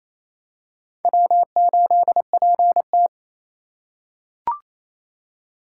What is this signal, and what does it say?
Morse code, heard as a keyed tone, about 28 wpm, 700 Hz. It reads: W8PT
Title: USA Call Signs - 4 Characters - 4x Character Spacing - 28wpm